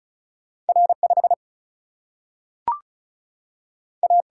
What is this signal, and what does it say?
Morse code, heard as a keyed tone, about 36 wpm, 700 Hz. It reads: R5 A
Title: Single-Letter Single-Number 35wpm